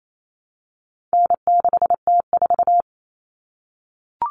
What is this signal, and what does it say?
Morse code, heard as a keyed tone, about 28 wpm, 700 Hz. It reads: N6T4
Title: All Cut Numbers - 4-Digit 28wpm